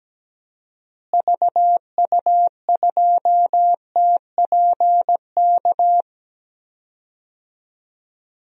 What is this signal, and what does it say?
Morse code, heard as a keyed tone, about 17 wpm, 700 Hz. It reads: VU2TPK
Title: DXCC-100 Call Signs - 6 Characters - Repeated 3x - 17wpm